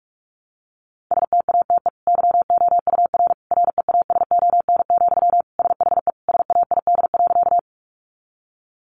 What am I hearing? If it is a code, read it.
Morse code, heard as a keyed tone, about 44 wpm, 700 Hz. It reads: STATE YOUR REASON, SHE SAID.
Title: Sentences from Top 700 Words 45wpm